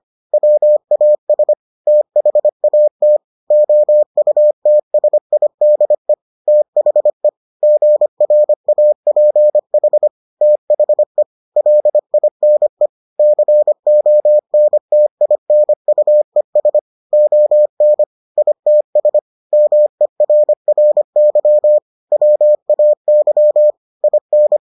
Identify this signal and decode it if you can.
Morse code, heard as a keyed tone, about 25 wpm, 600 Hz. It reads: WAS THAT OUTSIDE THE GRAPH THE LINE CONTINUES ON ITS MERRY WAY IN